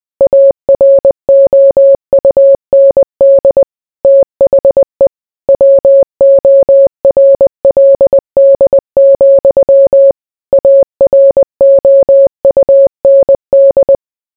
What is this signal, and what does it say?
Morse code, heard as a keyed tone, about 20 wpm, 550 Hz. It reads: AROUND THE WORLD, AROUND